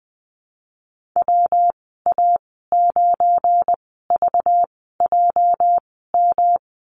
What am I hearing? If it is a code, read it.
Morse code, heard as a keyed tone, about 20 wpm, spaced as for 14 wpm, 700 Hz. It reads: WA9VJM
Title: 245 Lesson 39g - Letter z - Call Signs 20wpm